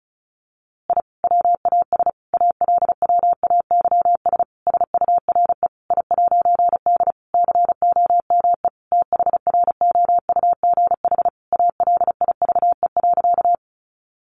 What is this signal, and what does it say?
Morse code, heard as a keyed tone, about 34 wpm, 700 Hz. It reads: I WAS ALWAYS SURE I'D COME THROUGH ALIVE.
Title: A Sound Of Thunder 35wpm